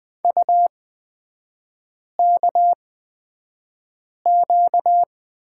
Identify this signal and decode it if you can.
Morse code, heard as a keyed tone, about 20 wpm, spaced as for 5 wpm, 700 Hz. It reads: UKQ
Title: 255 Lesson 41e - Period - Three Characters 20wpm